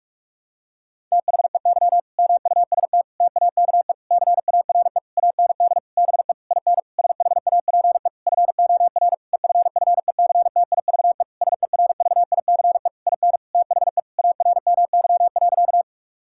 Morse code, heard as a keyed tone, about 46 wpm, 700 Hz. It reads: THEY MUST TAKE CARE AND BE IN SHAPE FOR EFFECTIVE SERVICE IN THE ARMY.
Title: Sentences from Top 1000 Words - Review Part 6 - 45wpm